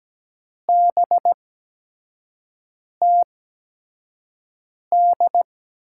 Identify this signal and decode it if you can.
Morse code, heard as a keyed tone, about 17 wpm, spaced as for 5 wpm, 700 Hz. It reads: BTD
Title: USA Call Signs - 8x Character Spacing - 17wpm